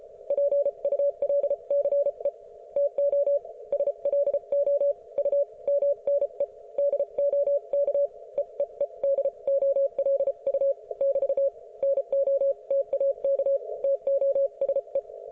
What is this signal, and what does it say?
Morse code, heard as a keyed tone, about 34 wpm, spaced as for 29 wpm, 570 Hz. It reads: PULCE TO SLO U MNE DO KEEE DOLU <BT> NO TAK TO SE